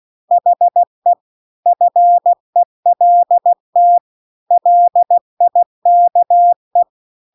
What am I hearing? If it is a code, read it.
Morse code, heard as a keyed tone, about 16 wpm, 700 Hz. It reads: HE FELT LIKE
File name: CW-The-Big-Twoheaded-River-2-16WPM